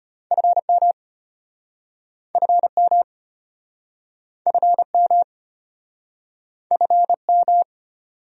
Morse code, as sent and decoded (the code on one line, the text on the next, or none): ..-. -- / ..-. -- / ..-. -- / ..-. --
FM FM FM FM